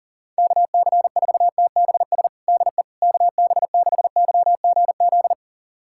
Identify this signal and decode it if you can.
Morse code, heard as a keyed tone, about 37 wpm, 700 Hz. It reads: KC4TBS DE KB6YGZ
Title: USA Callsign DE Callsign - 6 Characters - 40wpm